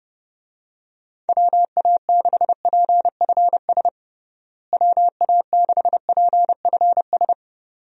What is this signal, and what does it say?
Morse code, heard as a keyed tone, about 30 wpm, 700 Hz. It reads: WA6PFS WA6PFS
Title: USA Call Signs - 6 Characters - Repeated 2x - 30wpm